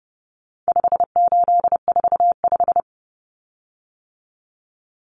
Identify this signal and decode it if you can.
Morse code, heard as a keyed tone, about 28 wpm, 700 Hz. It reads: 5845
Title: Common Cut Numbers - 4-Digit 30wpm